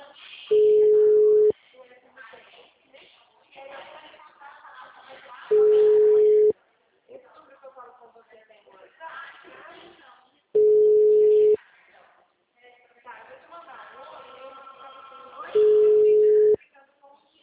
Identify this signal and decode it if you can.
Ringback tone (Europe, 425 Hz), with speech.